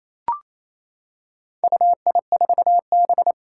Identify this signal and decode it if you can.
Morse code, heard as a keyed tone, about 27 wpm, 700 Hz. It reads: UI4B
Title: Four Challenging Letter-Number Combinations 28wpm